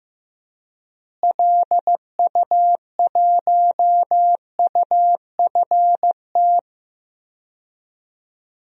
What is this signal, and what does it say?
Morse code, heard as a keyed tone, about 15 wpm, 700 Hz. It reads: LU1UFT
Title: DXCC-100 Call Signs - 6 Characters - Repeated 2x - 15wpm